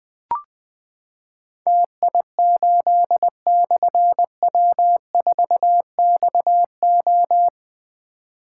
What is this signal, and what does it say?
Morse code, heard as a keyed tone, about 20 wpm, 700 Hz. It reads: TI8/W4XO
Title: Difficult Call Signs 20wpm